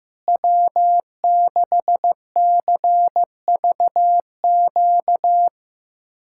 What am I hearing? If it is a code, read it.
Morse code, heard as a keyed tone, about 15 wpm, 700 Hz. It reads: W6CVQ
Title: USA Callsign DE Callsign 15wpm